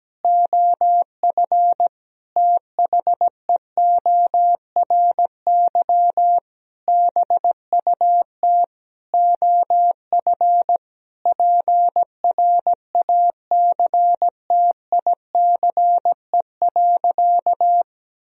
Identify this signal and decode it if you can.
Morse code, heard as a keyed tone, about 17 wpm, 700 Hz. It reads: OF THEORY BUT OF PRACTICE.
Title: Sentences from Top 1000 Words - Review Part 4 - 17wpm